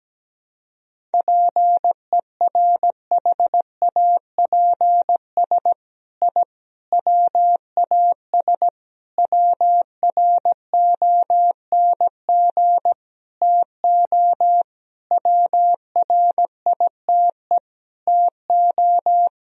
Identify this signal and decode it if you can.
Morse code, heard as a keyed tone, about 17 wpm, 700 Hz. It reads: PERHAPS I WAS WRONG TO WRITE TO
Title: Sentences from Top 1000 Words - Review Part 1 - 17wpm